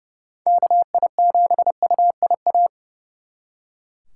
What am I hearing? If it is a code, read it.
Morse code, heard as a keyed tone, about 35 wpm, 700 Hz. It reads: KI7UIA
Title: Speed Racing - USA Call Signs 30wpm